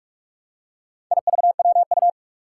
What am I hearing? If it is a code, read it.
Morse code, heard as a keyed tone, about 44 wpm, 700 Hz. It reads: IVWU
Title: Four Challenging Letter-Number Combinations 45wpm